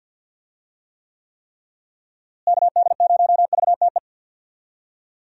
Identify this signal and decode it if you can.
Morse code, heard as a keyed tone, about 51 wpm, 700 Hz. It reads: KD0VTE
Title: USA Call Signs - 6 Characters - Repeated 3x - 50wpm